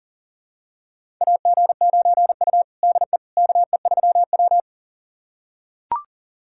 Morse code, heard as a keyed tone, about 40 wpm, 700 Hz. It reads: AG9U DE KE3W
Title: USA Callsign DE Callsign - 4 Characters - 40wpm